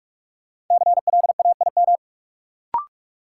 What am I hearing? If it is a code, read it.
Morse code, heard as a keyed tone, about 46 wpm, 700 Hz. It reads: CLAIM
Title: Sentences from Top 1500 Words - New Word Drill - 45wpm